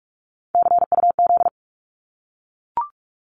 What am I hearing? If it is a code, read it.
Morse code, heard as a keyed tone, about 44 wpm, 700 Hz. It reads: CUZ
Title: Abbreviations Repeated 3x - 45wpm